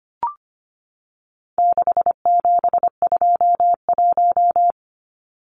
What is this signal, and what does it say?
Morse code, heard as a keyed tone, about 25 wpm, 700 Hz. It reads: 6721
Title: Numbers - 4-Digit 25wpm